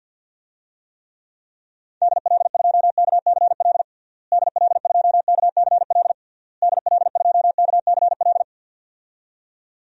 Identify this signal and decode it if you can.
Morse code, heard as a keyed tone, about 47 wpm, 700 Hz. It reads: DL2KCL DL2KCL DL2KCL
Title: DXCC-100 Call Signs - 6 Characters - Repeated 3x - 50wpm